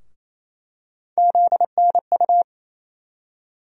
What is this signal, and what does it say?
Morse code, heard as a keyed tone, about 28 wpm, 700 Hz. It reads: ZNU
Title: USA Call Sign Postfixes - 3 Letters 28wpm